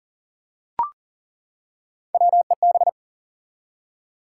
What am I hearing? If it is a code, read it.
Morse code, heard as a keyed tone, about 40 wpm, 700 Hz. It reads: WEB
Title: Three Letter Words - 40wpm